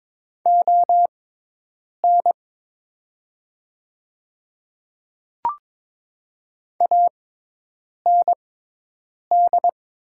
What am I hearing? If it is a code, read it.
Morse code, heard as a keyed tone, about 22 wpm, spaced as for 8 wpm, 700 Hz. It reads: ON AND
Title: 6x Character Spacing - Top 100 Words - 22wpm